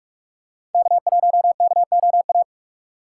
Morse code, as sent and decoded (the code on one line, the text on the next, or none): -.- .---- -.- --- .-
K1KOA